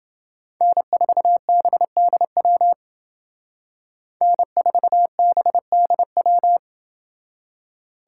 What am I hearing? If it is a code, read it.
Morse code, heard as a keyed tone, about 31 wpm, 700 Hz. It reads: N4BDW N4BDW